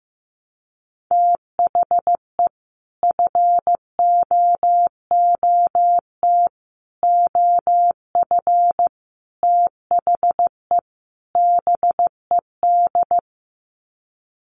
Morse code, heard as a keyed tone, about 15 wpm, 700 Hz. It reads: THE FOOT OF THE BED
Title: Sets of 5 Words Encore2 15wpm